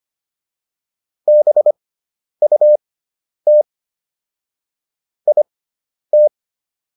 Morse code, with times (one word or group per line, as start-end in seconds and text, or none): 1.3-3.6 BUT
5.3-6.3 IT